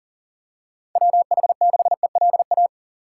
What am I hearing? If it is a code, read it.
Morse code, heard as a keyed tone, about 40 wpm, 700 Hz. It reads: WH6ELA
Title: USA Call Signs - 6 Characters - 4x Character Spacing - 40wpm